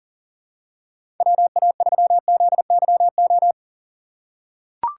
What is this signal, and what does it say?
Morse code, heard as a keyed tone, about 40 wpm, 700 Hz. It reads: WA3ZYO
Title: USA Call Signs - 6 Characters - 6x Character Spacing - 40wpm